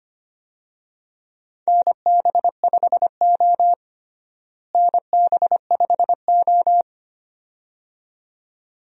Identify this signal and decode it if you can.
Morse code, heard as a keyed tone, about 25 wpm, 700 Hz. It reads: NB5O NB5O
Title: USA Call Signs - 4 Characters - Repeated 2x - 25wpm